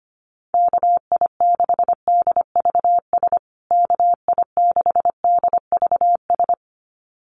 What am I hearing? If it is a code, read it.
Morse code, heard as a keyed tone, about 25 wpm, 700 Hz. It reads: KI6DVS KI6DVS
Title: USA Call Signs Repeated 2x 25wpm